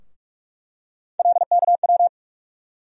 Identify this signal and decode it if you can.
Morse code, heard as a keyed tone, about 46 wpm, 700 Hz. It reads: LKW
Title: USA Call Sign Postfixes - 3 Letters 45wpm